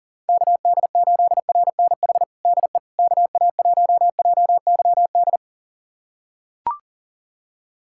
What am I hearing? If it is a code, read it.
Morse code, heard as a keyed tone, about 37 wpm, 700 Hz. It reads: KD8RNH DE KA1JYD
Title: USA Callsign DE Callsign 40wpm